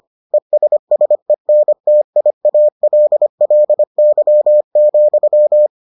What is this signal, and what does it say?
Morse code, heard as a keyed tone, about 25 wpm, 600 Hz. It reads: ESSENTIALLY,